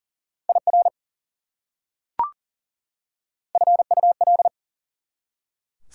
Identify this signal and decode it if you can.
Morse code, heard as a keyed tone, about 40 wpm, 700 Hz. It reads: IR FUL
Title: Prefixes and Suffixes 40wpm